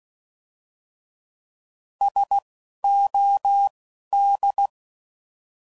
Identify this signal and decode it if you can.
Morse code, heard as a keyed tone, about 16 wpm, spaced as for 12 wpm, 800 Hz. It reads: SOD